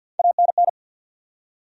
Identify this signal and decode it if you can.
Morse code, heard as a keyed tone, about 53 wpm, 700 Hz. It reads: ANN
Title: USA Common Names - Max 3 Letters - 50wpm